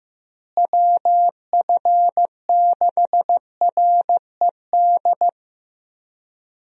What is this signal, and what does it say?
Morse code, heard as a keyed tone, about 15 wpm, 700 Hz. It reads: WF6RED
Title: USA Call Signs - 6 Characters - Repeated 3x - 15wpm